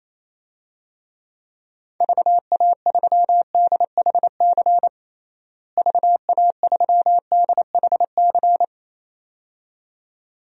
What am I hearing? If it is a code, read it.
Morse code, heard as a keyed tone, about 28 wpm, 700 Hz. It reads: VA3DHC VA3DHC
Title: DXCC-100 Call Signs - 6 Characters - Repeated 2x - 28wpm